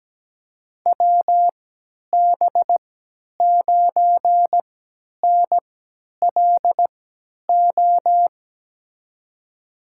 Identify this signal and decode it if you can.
Morse code, heard as a keyed tone, about 17 wpm, spaced as for 10 wpm, 700 Hz. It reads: WB9NLO
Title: USA Call Signs - 6 Characters - 3x Character Spacing - 17wpm